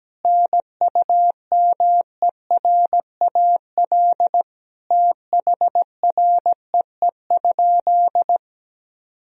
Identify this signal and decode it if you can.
Morse code, heard as a keyed tone, about 17 wpm, 700 Hz. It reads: NUMERAL THREE?
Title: Sentences from Top 1000 Words - Review Part 2 - 17wpm